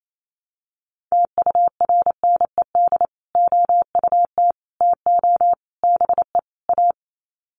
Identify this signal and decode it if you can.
Morse code, heard as a keyed tone, about 28 wpm, 700 Hz. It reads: TURNED OUT TO BE A
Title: Sets of 5 Words 28wpm